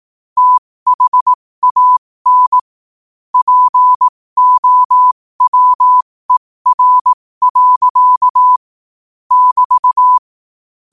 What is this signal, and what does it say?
Morse code, heard as a keyed tone, about 18 wpm, spaced as for 15 wpm, 1000 Hz. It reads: THAN POWER. <BT>
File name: qotdfiles / 15wpm